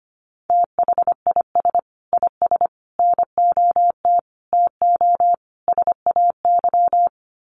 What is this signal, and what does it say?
Morse code, heard as a keyed tone, about 25 wpm, 700 Hz. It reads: THIS IS NOT TO SAY
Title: Sets of 5 Words Encore 25wpm